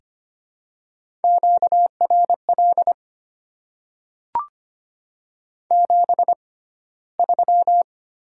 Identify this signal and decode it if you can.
Morse code, heard as a keyed tone, about 25 wpm, spaced as for 15 wpm, 700 Hz. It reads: QRL 73
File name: Top 20 QSO Elements - 6x Character Spacing - 25wpm